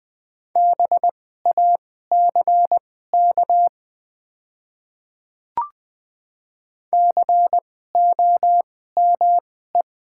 Morse code, heard as a keyed tone, about 20 wpm, spaced as for 14 wpm, 700 Hz. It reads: BACK COME
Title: 2x Character Spacing - Top 200 Words - 20wpm